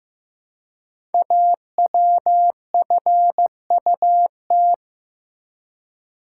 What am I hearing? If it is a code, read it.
Morse code, heard as a keyed tone, about 15 wpm, 700 Hz. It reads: AWFUT
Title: Code Groups - 5 Letters - 15wpm